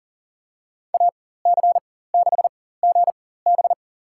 Morse code, heard as a keyed tone, about 38 wpm, spaced as for 18 wpm, 700 Hz. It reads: AC6GB
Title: USA Call Signs - 5 Characters - 4x Character Spacing - 40wpm